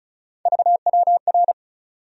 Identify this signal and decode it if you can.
Morse code, heard as a keyed tone, about 36 wpm, 700 Hz. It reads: VWR
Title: USA Call Sign Postfixes - 3 Letters 35wpm